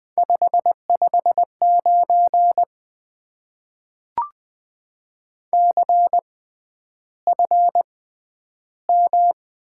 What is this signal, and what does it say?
Morse code, heard as a keyed tone, about 20 wpm, spaced as for 10 wpm, 700 Hz. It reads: 559 CFM
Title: Top 90 QSO Elements - 6x Character Spacing - 20wpm